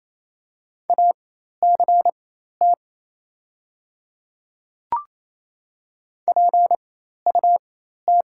Morse code, heard as a keyed tone, about 28 wpm, spaced as for 13 wpm, 700 Hz. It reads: ACT PUT
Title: 4x Character Spacing - Top 500 Words - 28wpm